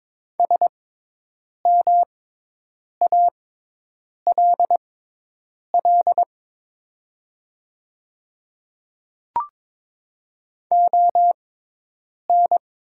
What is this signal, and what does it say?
Morse code, heard as a keyed tone, about 22 wpm, spaced as for 8 wpm, 700 Hz. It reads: SMALL ON
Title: 6x Character Spacing - Top 200 Words - 22wpm